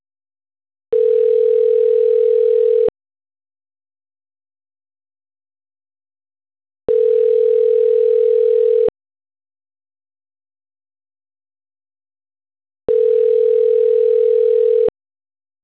Ringback tone (North America), 440 + 480 Hz.